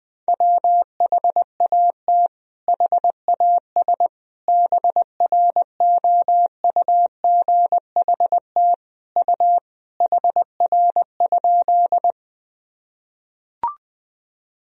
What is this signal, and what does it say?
Morse code, heard as a keyed tone, about 20 wpm, 700 Hz. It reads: WHAT HAS BROUGHT U HR?
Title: Sentences from Top 500 Words and Abbreviations 20wpm